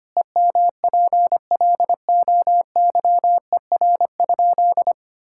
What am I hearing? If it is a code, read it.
Morse code, heard as a keyed tone, about 25 wpm, 700 Hz. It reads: EMPLOYER?
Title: Sentences from Top 1600 Words 25wpm